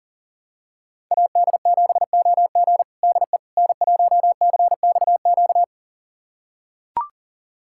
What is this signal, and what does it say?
Morse code, heard as a keyed tone, about 37 wpm, 700 Hz. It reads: AD7OG DE N1CXQ
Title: USA Callsign DE Callsign - 5 Characters - 40wpm